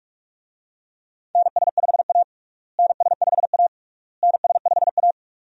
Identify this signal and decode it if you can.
Morse code, heard as a keyed tone, about 46 wpm, 700 Hz. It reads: NS5A NS5A NS5A
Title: USA Call Signs - 4 Characters - Repeated 3x - 45wpm